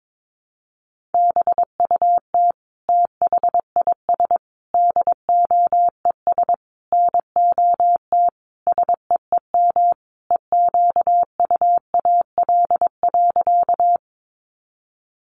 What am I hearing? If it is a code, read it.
Morse code, heard as a keyed tone, about 22 wpm, 700 Hz. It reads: BUT THIS DOES NOT SEEM EQUAL.
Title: Sentences from Top 1400 Words 22wpm